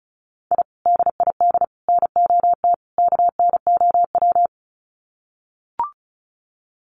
Morse code, heard as a keyed tone, about 34 wpm, 700 Hz. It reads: I DID NOT KNOW
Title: Sets of 4 Words 35wpm